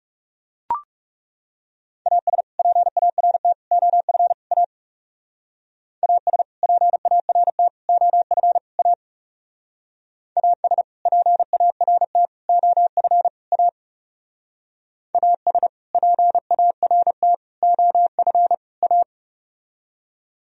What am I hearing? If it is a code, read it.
Morse code, heard as a keyed tone, about 35 wpm, 690 Hz. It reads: AS PART OF A AS PART OF A AS PART OF A AS PART OF A
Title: Speed-Racing - Sets of 4 Words 30wpm